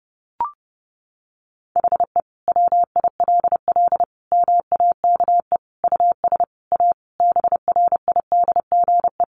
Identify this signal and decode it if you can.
Morse code, heard as a keyed tone, about 30 wpm, 700 Hz. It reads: HE WILL MAKE US A BRIDGE
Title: Sentences from Top 1200 Words 30wpm